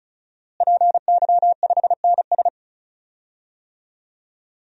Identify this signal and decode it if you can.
Morse code, heard as a keyed tone, about 36 wpm, 700 Hz. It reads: PY5NS
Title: DXCC-100 Call Signs - 5 Characters - Repeated 3x - 35wpm